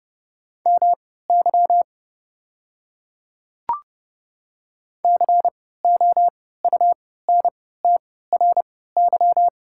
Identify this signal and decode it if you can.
Morse code, heard as a keyed tone, about 30 wpm, spaced as for 17 wpm, 700 Hz. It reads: MY COUNTRY
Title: 3x Character Spacing - Top 500 Words - 30wpm